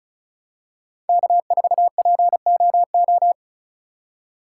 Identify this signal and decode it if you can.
Morse code, heard as a keyed tone, about 34 wpm, 700 Hz. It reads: K4POO
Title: USA Call Signs - 5 Characters - 3x Character Spacing - 35wpm